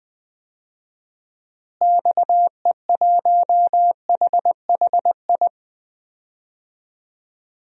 Morse code, heard as a keyed tone, about 20 wpm, 700 Hz. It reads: XE1HHI
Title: DXCC-100 Call Signs - 6 Characters - Repeated 3x - 20wpm